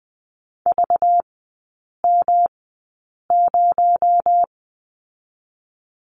Morse code, heard as a keyed tone, about 20 wpm, 700 Hz. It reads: V M 0
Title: 218 Lesson 35e - Number 0 - Three Characters 20wpm